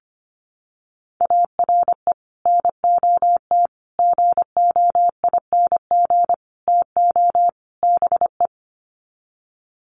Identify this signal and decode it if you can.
Morse code, heard as a keyed tone, about 25 wpm, 700 Hz. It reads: ARE NOT GOING TO BE
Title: Sets of 5 Words Encore 25wpm